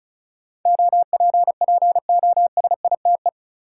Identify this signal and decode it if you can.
Morse code, heard as a keyed tone, about 34 wpm, 700 Hz. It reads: OPPOSITE
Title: Sentences from Top 1300 Words - New Word Drill Alt - 35wpm